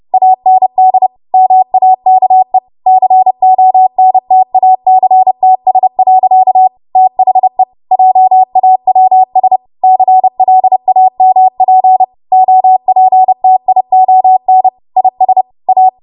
Morse code, heard as a keyed tone, about 30 wpm, 750 Hz. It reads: AND MAKE CONTACTS. THE JAWS CLAMP OPTION IS A
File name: Morse / 30WPM